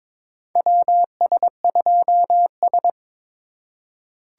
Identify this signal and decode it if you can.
Morse code, heard as a keyed tone, about 22 wpm, 700 Hz. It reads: WS2S
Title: USA Call Signs - 4 Characters - 2x Character Spacing - 22wpm